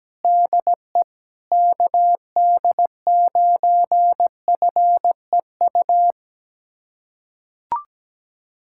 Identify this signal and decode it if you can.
Morse code, heard as a keyed tone, about 17 wpm, 700 Hz. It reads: DE KD9FEU
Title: USA Callsign DE Callsign - 6 Characters - 17wpm